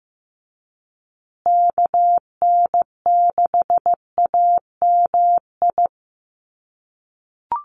Morse code, heard as a keyed tone, about 15 wpm, 700 Hz. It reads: KN6AMI
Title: USA Call Signs Repeated 3x 15wpm